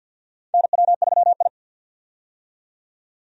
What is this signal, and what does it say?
Morse code, heard as a keyed tone, about 52 wpm, 700 Hz. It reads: NW3I
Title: USA Call Signs - 4 Characters - 50wpm